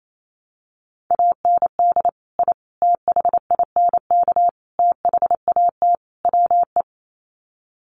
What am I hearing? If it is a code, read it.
Morse code, heard as a keyed tone, about 28 wpm, 700 Hz. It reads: AND I THINK THAT WE
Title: Sets of 5 Words Encore2 28wpm